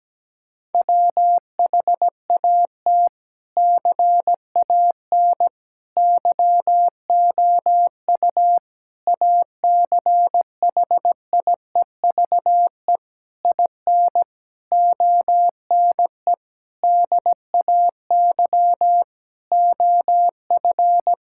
Morse code, heard as a keyed tone, about 17 wpm, 700 Hz. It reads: WHAT CAN YOU ACHIEVE IN ONE DAY OF
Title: Sentences from Top 1000 Words - Review Part 6 - 17wpm